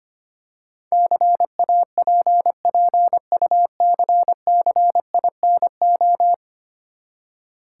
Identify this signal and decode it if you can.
Morse code, heard as a keyed tone, about 25 wpm, 690 Hz. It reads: CAPPUCCINO
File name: practice / 25wpm / Easy Words with Hints - 25wpm